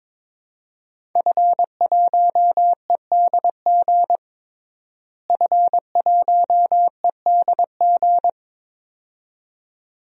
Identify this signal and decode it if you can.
Morse code, heard as a keyed tone, about 22 wpm, 700 Hz. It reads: F1EDG F1EDG
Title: DXCC-100 Call Signs - 5 Characters - Repeated 2x - 22wpm